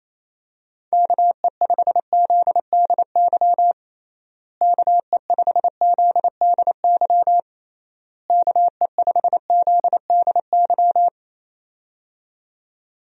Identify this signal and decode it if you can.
Morse code, heard as a keyed tone, about 28 wpm, 700 Hz. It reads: KE5ZDY KE5ZDY KE5ZDY
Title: USA Call Signs - 6 Characters - Repeated 3x - 28wpm